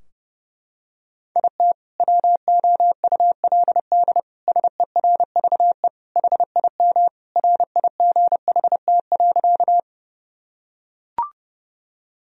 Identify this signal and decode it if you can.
Morse code, heard as a keyed tone, about 30 wpm, 700 Hz. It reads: IT WOULD SERVE HIM RIGHT.
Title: Sentences from Top 1000 Words - Review Part 2 - 30wpm